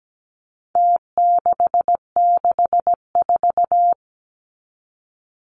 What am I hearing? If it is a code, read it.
Morse code, heard as a keyed tone, about 17 wpm, 700 Hz. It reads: T664
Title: Numbers - Contest Style - 4-Digit 17wpm